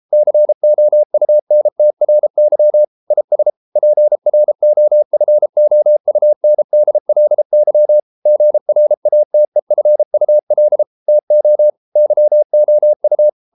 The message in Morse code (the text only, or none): COUNTRY IS PROFOUNDLY GRATEFUL TO YOU